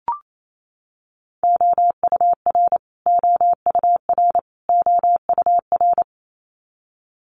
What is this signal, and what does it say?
Morse code, heard as a keyed tone, about 28 wpm, 700 Hz. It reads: OUR OUR OUR
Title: Top 500 Words Repeated 28wpm